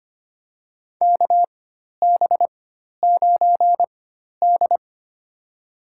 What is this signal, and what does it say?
Morse code, heard as a keyed tone, about 25 wpm, spaced as for 12 wpm, 700 Hz. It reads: KB9D
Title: USA Call Signs - 4 Characters - 4x Character Spacing - 25wpm